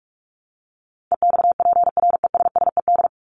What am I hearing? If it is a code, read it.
Morse code, heard as a keyed tone, about 44 wpm, 700 Hz. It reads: EXPRESSED
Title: ED Suffix - Top 100 Words with Base Words 45wpm